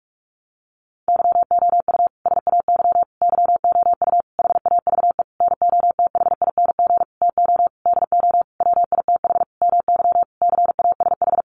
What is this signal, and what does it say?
Morse code, heard as a keyed tone, about 44 wpm, 700 Hz. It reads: YOU SAY YOU HAVE NOTHING TO DO WITH MY CASH
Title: Sentences from Top 700 Words 45wpm